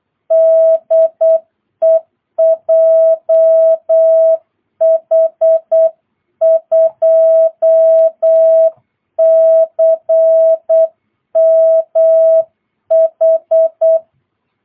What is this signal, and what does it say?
Morse code, heard as a keyed tone, about 8 wpm, 660 Hz. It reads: DE JH2CMH
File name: jh8cbh / 2022cw